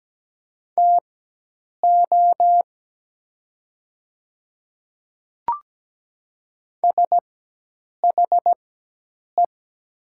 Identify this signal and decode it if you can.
Morse code, heard as a keyed tone, about 17 wpm, spaced as for 8 wpm, 690 Hz. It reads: TO SHE
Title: 4x Character Spacing - Top 100 Words - 17wpm